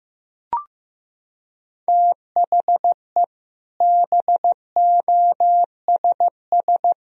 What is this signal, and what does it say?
Morse code, heard as a keyed tone, about 15 wpm, 700 Hz. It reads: THE BOSS
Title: Sentences from Top 1000 Words - Review Part 4 - 15wpm